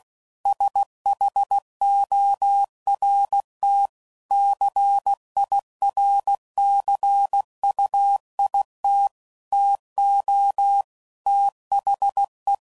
Morse code, heard as a keyed tone, about 16 wpm, 800 Hz. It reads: SHORT CIRCUIT TO THE
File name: CW Code Groups @15WPM / Technical Subjects / XC